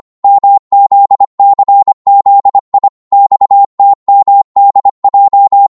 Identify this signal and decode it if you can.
Morse code, heard as a keyed tone, about 25 wpm, 800 Hz. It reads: MZCZI XTMDJ
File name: Morse / ltr-25wpm